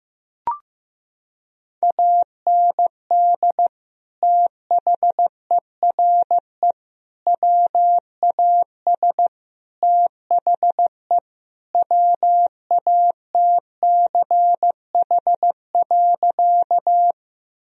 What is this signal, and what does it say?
Morse code, heard as a keyed tone, about 15 wpm, 700 Hz. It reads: AND THERE WAS THE WATCH.
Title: Sentences from Top 1000 Words - Review Part 1 - 15wpm